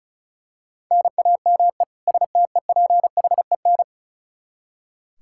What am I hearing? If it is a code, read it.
Morse code, heard as a keyed tone, about 34 wpm, 700 Hz. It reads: NAME STEPHEN
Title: Practice QSO - Exchange 1 - Beginning - 35wpm